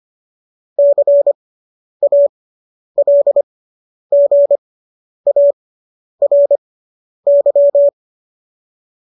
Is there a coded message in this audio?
Yes: Morse code from 0.8 to 7.9 s, heard as a keyed tone, about 25 wpm, spaced as for 10 wpm, 570 Hz. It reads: CALGARY